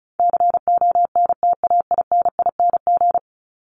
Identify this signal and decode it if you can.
Morse code, heard as a keyed tone, about 34 wpm, 700 Hz. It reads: CONTAINING
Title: ING Suffix - Top 100 Words with Base Words 35wpm